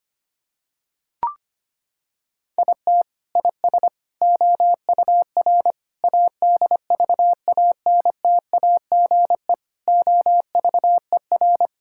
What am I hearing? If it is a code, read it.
Morse code, heard as a keyed tone, about 25 wpm, 700 Hz. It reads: IT IS OUR ADVANTAGE OVER
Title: Sentences from Top 1000 Words - Review Part 6 - 25wpm